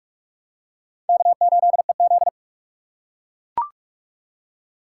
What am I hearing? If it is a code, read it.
Morse code, heard as a keyed tone, about 43 wpm, 700 Hz. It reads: K8EZ